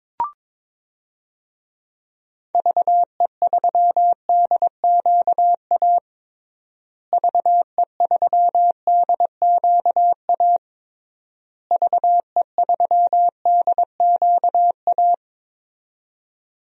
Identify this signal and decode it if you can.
Morse code, heard as a keyed tone, about 22 wpm, 700 Hz. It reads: VE3DQA VE3DQA VE3DQA
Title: DXCC-100 Call Signs - 6 Characters - Repeated 3x - 22wpm